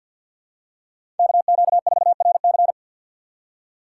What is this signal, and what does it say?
Morse code, heard as a keyed tone, about 50 wpm, 700 Hz. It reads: KQ4RC
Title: USA Call Signs - 5 Characters - 2x Character Spacing - 50wpm